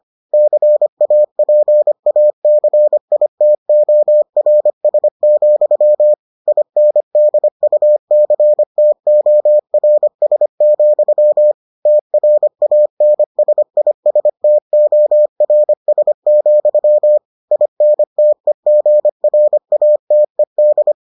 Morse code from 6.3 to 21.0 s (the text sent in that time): INDUCTORS, TRANSISTORS, INTEGRATED